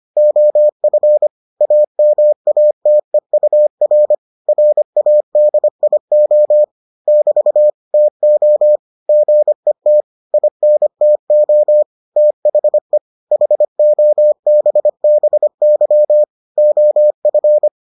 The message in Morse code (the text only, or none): OF AMATEUR RADIO <BT> TO GET INTO THE HOBBY OF